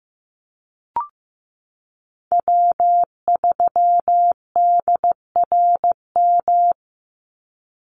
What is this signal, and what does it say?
Morse code, heard as a keyed tone, about 15 wpm, 700 Hz. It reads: W3DRM
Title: USA Call Signs 15wpm